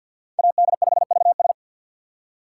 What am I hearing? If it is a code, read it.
Morse code, heard as a keyed tone, about 48 wpm, 700 Hz. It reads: AD5VS
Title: USA Call Signs - 5 Characters - 6x Character Spacing - 50wpm